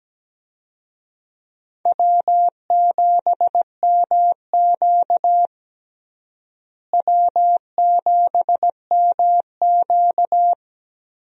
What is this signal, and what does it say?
Morse code, heard as a keyed tone, about 17 wpm, 700 Hz. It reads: W7MQ W7MQ
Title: USA Call Signs - 4 Characters - Repeated 3x - 17wpm